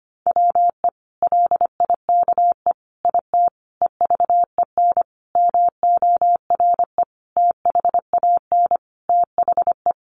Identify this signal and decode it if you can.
Morse code, heard as a keyed tone, about 25 wpm, 700 Hz. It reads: WE LIKE IT EVEN MORE THAN THE
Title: Sentences from Top 1100 Words 25wpm